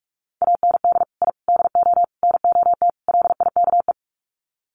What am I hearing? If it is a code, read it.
Morse code, heard as a keyed tone, about 44 wpm, 700 Hz. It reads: AND I DO NOT LIKE
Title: Sets of 5 Words Encore2 45wpm